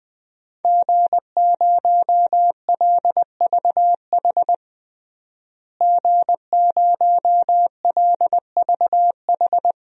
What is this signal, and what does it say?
Morse code, heard as a keyed tone, about 20 wpm, 700 Hz. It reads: G0LVH G0LVH
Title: DXCC-100 Call Signs - 5 Characters - Repeated 2x - 20wpm